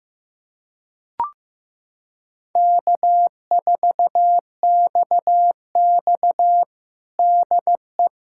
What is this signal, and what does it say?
Morse code, heard as a keyed tone, about 15 wpm, 700 Hz. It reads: K4XX DE
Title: USA Callsign DE Callsign - 4 Characters - 15wpm